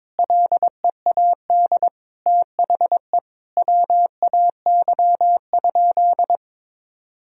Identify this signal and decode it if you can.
Morse code, heard as a keyed tone, about 22 wpm, 700 Hz. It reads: LEAD THE WAY?
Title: Sentences from Top 1000 Words - Review Part 2 - 22wpm